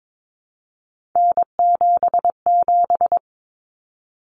Morse code, heard as a keyed tone, about 22 wpm, 700 Hz. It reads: N77
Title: Common Cut Numbers - 3-Digit 22wpm